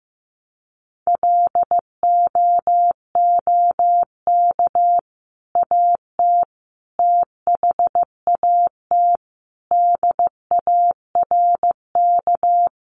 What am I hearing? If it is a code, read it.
Morse code, heard as a keyed tone, about 15 wpm, 700 Hz. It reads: LOOK AT THAT DARK
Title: Sentences from Top 500 Words 15wpm